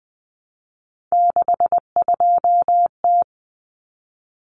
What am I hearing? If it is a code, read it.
Morse code, heard as a keyed tone, about 20 wpm, 700 Hz. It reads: 62T
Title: Common Cut Numbers - 3-Digit 20wpm